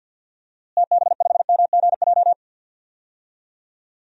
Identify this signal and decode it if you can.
Morse code, heard as a keyed tone, about 51 wpm, 700 Hz. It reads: TB5MGJ
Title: DXCC-100 Call Signs - 6 Characters - Repeated 3x - 50wpm